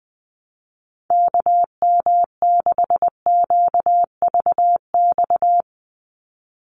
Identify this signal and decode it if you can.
Morse code, heard as a keyed tone, about 19 wpm, 700 Hz. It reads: KM6QVX
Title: USA Call Signs 20wpm